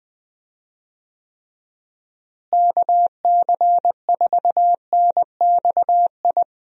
Morse code, heard as a keyed tone, about 21 wpm, 700 Hz. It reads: KC4NXI
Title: USA Call Signs - 6 Characters - Repeated 2x - 20wpm